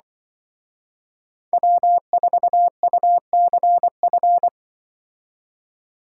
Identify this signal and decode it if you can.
Morse code, heard as a keyed tone, about 23 wpm, 700 Hz. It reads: W4UCF
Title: Speed Racing - USA 1x3 Call Signs 20wpm